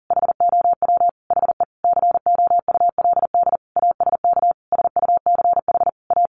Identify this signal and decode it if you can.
Morse code, heard as a keyed tone, about 40 wpm, 700 Hz. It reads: HOW HE COULD ASK SUCH A
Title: Sentences from Top 600 Words 40wpm